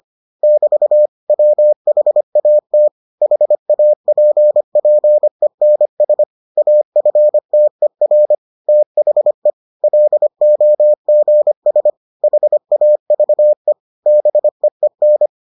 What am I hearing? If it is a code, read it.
Morse code, heard as a keyed tone, about 25 wpm, 600 Hz. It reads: <BT> WHAT HAPPENS AFTER THE LOGS HAVE BEEN